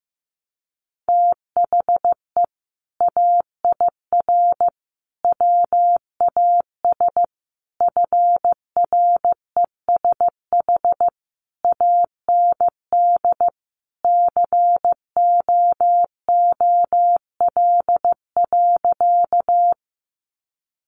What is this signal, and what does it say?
Morse code, heard as a keyed tone, about 15 wpm, 700 Hz. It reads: THE AIR WAS FRESH AND COOL.
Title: Sentences from Top 1200 Words 15wpm